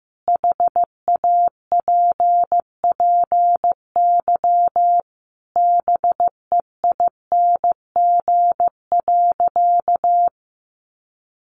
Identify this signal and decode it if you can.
Morse code, heard as a keyed tone, about 15 wpm, 700 Hz. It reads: HAPPY BEING.
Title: Sentences from Top 1000 Words 15wpm